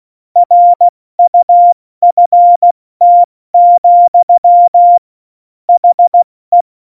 Morse code, heard as a keyed tone, about 16 wpm, 700 Hz. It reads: RUFT, HE
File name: Morsekurs / Witze / Witz 26. November 2019, 70 F80